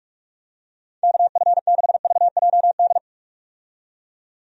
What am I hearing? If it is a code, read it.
Morse code, heard as a keyed tone, about 48 wpm, 700 Hz. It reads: KF6VJD
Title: USA Call Signs - 6 Characters - 45wpm